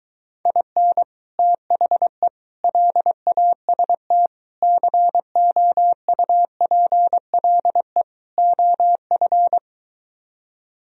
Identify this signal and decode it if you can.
Morse code, heard as a keyed tone, about 23 wpm, 700 Hz. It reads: IN THE LAST COUPLE OF